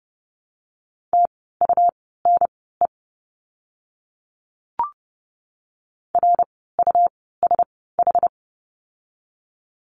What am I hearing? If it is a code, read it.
Morse code, heard as a keyed tone, about 30 wpm, spaced as for 17 wpm, 700 Hz. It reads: TUNE RUSH
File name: Lesson 17f - Letter u - Words 30wpm